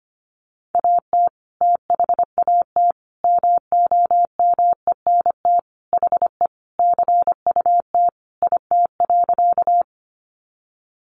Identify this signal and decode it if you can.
Morse code, heard as a keyed tone, about 25 wpm, 700 Hz. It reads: AT THAT MOMENT HE CUT IT.
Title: Sentences from Top 1000 Words 25wpm